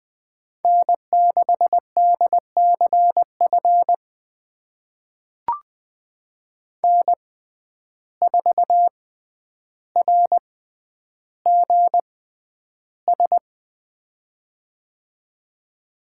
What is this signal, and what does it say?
Morse code, heard as a keyed tone, about 20 wpm, spaced as for 10 wpm, 700 Hz. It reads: N6DCF N4RGS
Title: USA Call Signs - 5 Characters - 6x Character Spacing - 20wpm